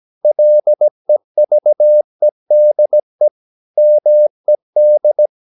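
Morse code, heard as a keyed tone, about 17 wpm, 600 Hz. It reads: LEVEDE MED
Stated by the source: Slow (QRS) Morse practice at 17 wpm, from CW med Gnister.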